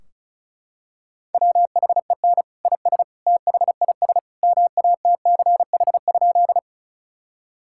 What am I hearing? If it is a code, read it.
Morse code, heard as a keyed tone, about 36 wpm, 700 Hz. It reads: WHEN IS THIS MATCH?
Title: Sentences from Top 1600 Words 35wpm